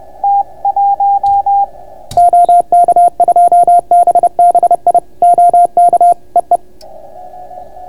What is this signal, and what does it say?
Morse code, heard as a keyed tone, about 30 wpm, 690 Hz. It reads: OK2BBI OK EE